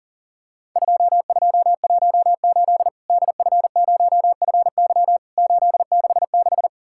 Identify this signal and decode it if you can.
Morse code, heard as a keyed tone, about 41 wpm, 700 Hz. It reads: 2218 DF0FY 866
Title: Worked All Europe DX Contest - QTC Exchange - 40wpm